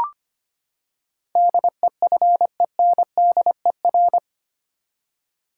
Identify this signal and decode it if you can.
Morse code, heard as a keyed tone, about 25 wpm, 700 Hz. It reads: DEFENDER
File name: Eight Letter Words - 25wpm